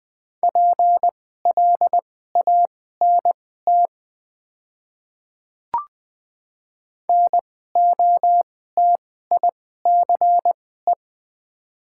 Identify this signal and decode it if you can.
Morse code, heard as a keyed tone, about 20 wpm, spaced as for 14 wpm, 700 Hz. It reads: PLANT NOTICE
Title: 2x Character Spacing - Top 500 Words - 20wpm